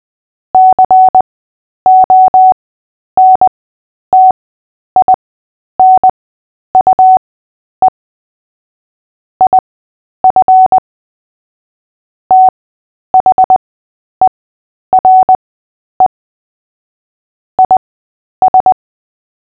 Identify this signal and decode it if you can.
Morse code, heard as a keyed tone, about 20 wpm, spaced as for 10 wpm, 750 Hz. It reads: CONTINUE IF THERE IS
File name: cw / fists / Short-Story-8-10wpm